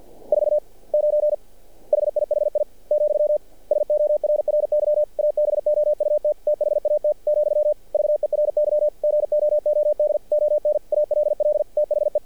Morse code, heard as a keyed tone, about 48 wpm, 600 Hz. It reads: 49 HIHI , SORRY ABOUT THAT , VERY GOOD ON ALL THE
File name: archived / CRSnet